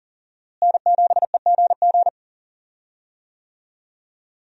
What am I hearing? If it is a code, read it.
Morse code, heard as a keyed tone, about 40 wpm, 700 Hz. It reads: N7EGG
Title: USA Call Signs - 5 Characters - Repeated 3x - 40wpm